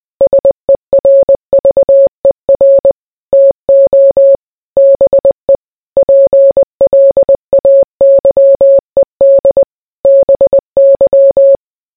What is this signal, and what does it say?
Morse code, heard as a keyed tone, about 20 wpm, 550 Hz. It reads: SERVER TO BE PLAYED BY